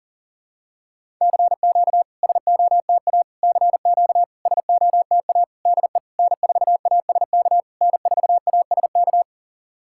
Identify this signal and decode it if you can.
Morse code, heard as a keyed tone, about 37 wpm, 690 Hz. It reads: CQ SOTA CQ SOTA DE N4ASK N4ASK
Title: SOTA QSO Practice 40wpm